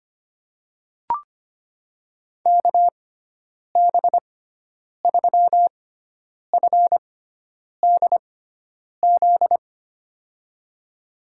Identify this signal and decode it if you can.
Morse code, heard as a keyed tone, about 25 wpm, spaced as for 9 wpm, 700 Hz. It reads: KB3FDZ
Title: USA Call Signs - 6 Characters - 6x Character Spacing - 25wpm